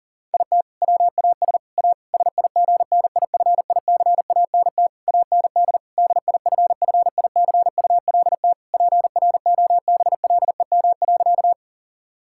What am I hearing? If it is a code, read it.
Morse code, heard as a keyed tone, about 40 wpm, 700 Hz. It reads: IT WAS A SIGNIFICANT AND DIFFICULT PROBLEM.
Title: Sentences from Top 1000 Words - Review Part 6 - 40wpm